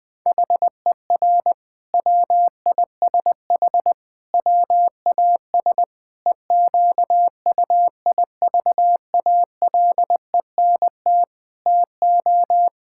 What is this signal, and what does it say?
Morse code, heard as a keyed tone, about 20 wpm, 700 Hz. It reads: HER WISH WAS EQUIVALENT TO
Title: Sentences from Top 1600 Words 20wpm